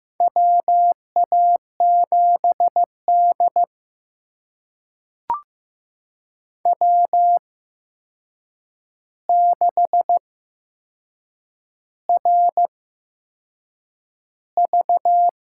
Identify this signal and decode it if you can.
Morse code, heard as a keyed tone, about 15 wpm, spaced as for 6 wpm, 700 Hz. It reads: WA7D W6RV
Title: USA Call Signs - 4 Characters - 8x Character Spacing - 15wpm